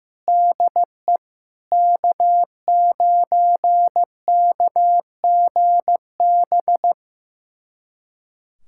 Morse code, heard as a keyed tone, about 16 wpm, 700 Hz. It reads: DE K9KGB
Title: USA Callsign DE Callsign - 5 Characters - 15wpm